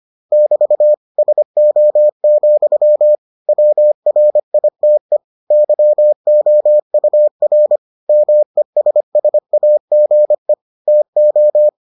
Morse code, heard as a keyed tone, about 25 wpm, 600 Hz. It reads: <BT> SO, WRITE YOUR MESSAGE TO